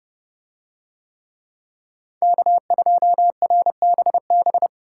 Morse code, heard as a keyed tone, about 29 wpm, 700 Hz. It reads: K2RBB